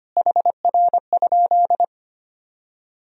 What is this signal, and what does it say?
Morse code, heard as a keyed tone, about 25 wpm, 700 Hz. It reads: HR?
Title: Sentences from Top 300 Words and Abbreviations 25wpm